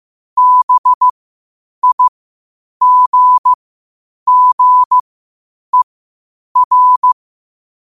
Morse code, heard as a keyed tone, about 15 wpm, spaced as for 8 wpm, 1000 Hz. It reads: BIGGER
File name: qotdfiles / 8wpm